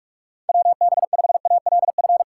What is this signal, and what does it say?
Morse code, heard as a keyed tone, about 41 wpm, 700 Hz. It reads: WB5ALF
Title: USA Call Signs - 6 Characters - 6x Character Spacing - 45wpm